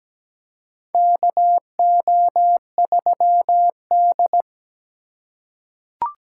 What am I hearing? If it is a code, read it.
Morse code, heard as a keyed tone, about 17 wpm, 700 Hz. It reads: KO3D
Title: USA Call Signs - 4 Characters - 2x Character Spacing - 17wpm